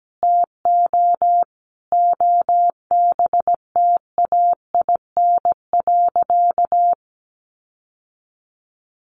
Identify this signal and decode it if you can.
Morse code, heard as a keyed tone, about 17 wpm, 700 Hz. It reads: TO OBTAIN.
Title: Sentences from Top 1300 Words 17wpm